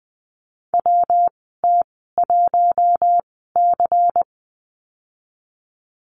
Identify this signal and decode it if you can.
Morse code, heard as a keyed tone, about 20 wpm, spaced as for 15 wpm, 700 Hz. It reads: WT1C
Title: 173 Lesson 28g - Number 9 - Call Signs 20wpm